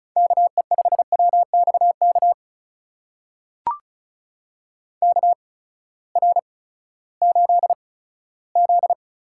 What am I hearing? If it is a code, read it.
Morse code, heard as a keyed tone, about 36 wpm, spaced as for 18 wpm, 700 Hz. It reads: KE5WXK KR8Z
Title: USA Call Signs - 6 Characters - 8x Character Spacing - 35wpm